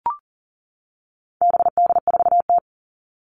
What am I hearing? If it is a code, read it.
Morse code, heard as a keyed tone, about 40 wpm, 700 Hz. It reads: BD4T
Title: All Cut Numbers - 4-Digit 40wpm